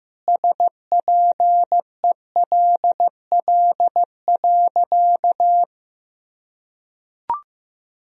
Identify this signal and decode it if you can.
Morse code, heard as a keyed tone, about 15 wpm, 700 Hz. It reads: SPELL.
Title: Sentences from Top 1000 Words - Review Part 1 - 15wpm